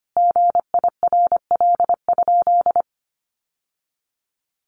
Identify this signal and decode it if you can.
Morse code, heard as a keyed tone, about 25 wpm, 700 Hz. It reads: GIRL?
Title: Sentences from Top 800 Words 25wpm